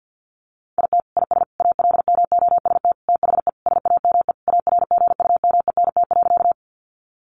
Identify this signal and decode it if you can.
Morse code, heard as a keyed tone, about 50 wpm, 700 Hz. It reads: IT IS ALMOST THE SAME ARGUMENT.
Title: Sentences from Top 1400 Words 50wpm